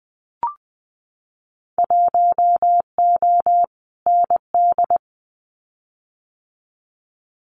Morse code, heard as a keyed tone, about 20 wpm, 700 Hz. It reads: 1O ND
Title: Winter Field Day Exchanges 20wpm